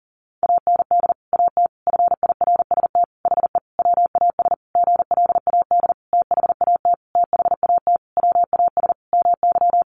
Morse code, heard as a keyed tone, about 40 wpm, 700 Hz. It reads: AND AT FIRST HE WAS GLAD THAT THAT WAS MY
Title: Sentences from Top 1300 Words 40wpm